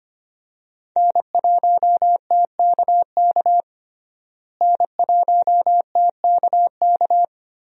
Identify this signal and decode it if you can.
Morse code, heard as a keyed tone, about 25 wpm, 700 Hz. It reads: N1TKK N1TKK
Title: USA Call Signs - 5 Characters - Repeated 3x - 25wpm